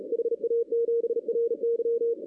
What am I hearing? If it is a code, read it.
Morse code, heard as a keyed tone, about 36 wpm, 470 Hz. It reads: HA7RY
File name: 2016-ARRL-DX-CW-K5TR